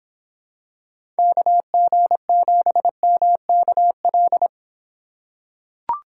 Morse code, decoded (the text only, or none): KG7MKL